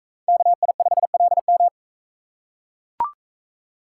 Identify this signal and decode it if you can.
Morse code, heard as a keyed tone, about 42 wpm, 700 Hz. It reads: KI5LM